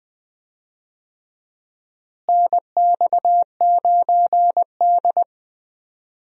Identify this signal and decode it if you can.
Morse code, heard as a keyed tone, about 20 wpm, 700 Hz. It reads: NX9D
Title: USA Call Signs - 4 Characters - Repeated 2x - 20wpm